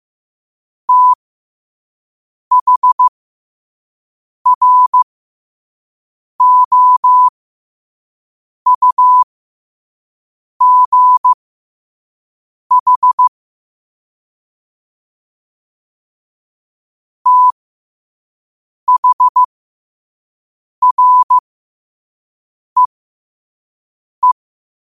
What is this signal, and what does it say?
Morse code, heard as a keyed tone, about 15 wpm, spaced as for 5 wpm, 1000 Hz. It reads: THROUGH THREE
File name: qotdfiles / 5wpm